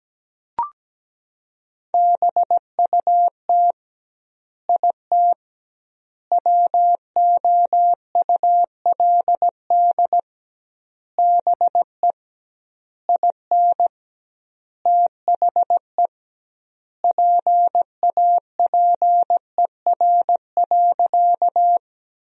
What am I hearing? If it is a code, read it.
Morse code, heard as a keyed tone, about 17 wpm, 700 Hz. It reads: BUT IT WOULD BE IN THE PAPER.
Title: 2x Word-Spacing - Sentences from Top 300 Words - 17wpm